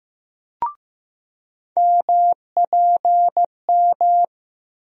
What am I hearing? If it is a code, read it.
Morse code, heard as a keyed tone, about 15 wpm, 700 Hz. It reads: MPM
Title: USA Call Sign Postfixes - 3 Letters 15wpm